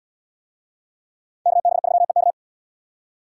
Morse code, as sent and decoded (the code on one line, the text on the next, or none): -.- -... ...-- . -.-
KB3EK